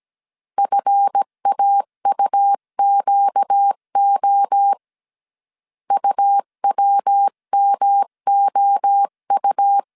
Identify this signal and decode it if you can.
Morse code, heard as a keyed tone, about 17 wpm, 800 Hz. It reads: FAUQO UWMOU
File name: CW / DET / 1 9 1500C S B 16wpm 2856